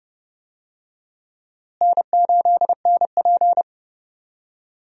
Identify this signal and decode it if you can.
Morse code, heard as a keyed tone, about 30 wpm, 700 Hz. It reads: N8NP
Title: USA Call Signs - 4 Characters - Repeated 3x - 30wpm